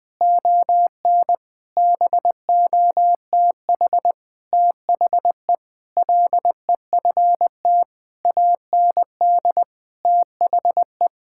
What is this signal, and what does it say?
Morse code, heard as a keyed tone, about 20 wpm, 700 Hz. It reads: ON BOTH THE LEFT AND THE
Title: Sentences with 7-Word Phrases - 20wpm